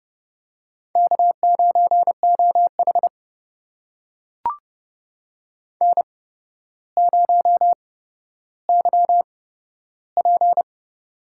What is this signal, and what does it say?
Morse code, heard as a keyed tone, about 31 wpm, spaced as for 13 wpm, 700 Hz. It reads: K9OH N0YP
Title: USA Call Signs - 4 Characters - 8x Character Spacing - 30wpm